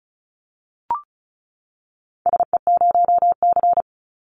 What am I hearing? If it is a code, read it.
Morse code, heard as a keyed tone, about 34 wpm, 700 Hz. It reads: SE0C
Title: Call Signs Repeated 3x - 35wpm